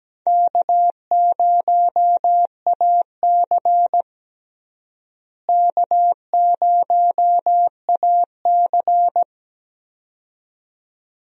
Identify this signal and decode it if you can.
Morse code, heard as a keyed tone, about 17 wpm, 700 Hz. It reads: K0AC K0AC
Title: USA Call Signs - 4 Characters - Repeated 2x - 17wpm